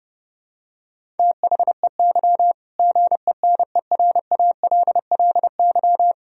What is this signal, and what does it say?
Morse code, heard as a keyed tone, about 30 wpm, 700 Hz. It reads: THEY GENERALLY
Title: Sentences from Top 1000 Words - Review Part 4 - 30wpm